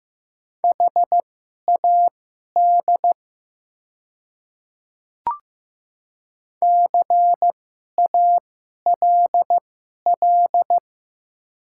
Morse code, heard as a keyed tone, about 15 wpm, spaced as for 11 wpm, 700 Hz. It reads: HAD CALL
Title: 2x Character Spacing - Top 100 Words - 15wpm